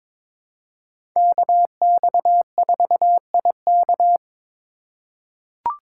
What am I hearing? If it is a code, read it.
Morse code, heard as a keyed tone, about 22 wpm, 700 Hz. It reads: KX4IK
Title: USA Call Signs - 5 Characters - 3x Character Spacing - 22wpm